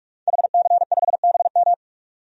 Morse code, heard as a keyed tone, about 46 wpm, 700 Hz. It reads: HC5BM
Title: DXCC-100 Call Signs - 5 Characters - 45wpm